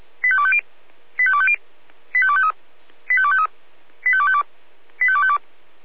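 Five-tone selective call (ZVEI), heard as quick runs of five tones, tones of about 70 ms. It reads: 85250 85250 85253 85253 85252 85252
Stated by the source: Five-tone ZVEI selective call received by SDR.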